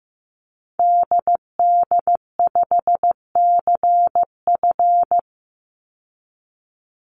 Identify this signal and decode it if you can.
Morse code, heard as a keyed tone, about 15 wpm, 700 Hz. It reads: DD5CF
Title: Call Signs Repeated 3x - 15wpm